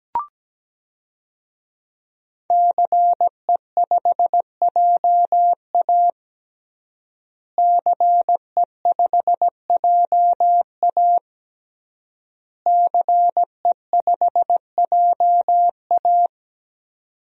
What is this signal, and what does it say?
Morse code, heard as a keyed tone, about 17 wpm, 700 Hz. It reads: CE5JA CE5JA CE5JA
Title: DXCC-100 Call Signs - 5 Characters - Repeated 3x - 17wpm